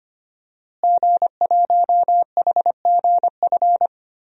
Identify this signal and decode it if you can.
Morse code, heard as a keyed tone, about 25 wpm, 700 Hz. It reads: G1HGF
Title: DXCC-100 Call Signs - 5 Characters - Repeated 2x - 25wpm